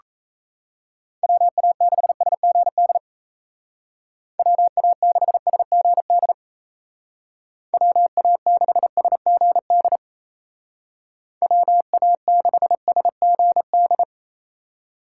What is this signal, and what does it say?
Morse code, heard as a keyed tone, about 34 wpm, 700 Hz. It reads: WA6SGD WA6SGD WA6SGD WA6SGD
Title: Speed Racing - USA Call Signs 28wpm